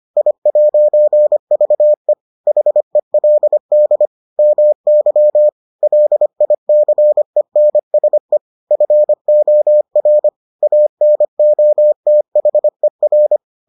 Morse code, heard as a keyed tone, about 25 wpm, 600 Hz. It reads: I'VE HELD MY LICENSE FOR ANOTHER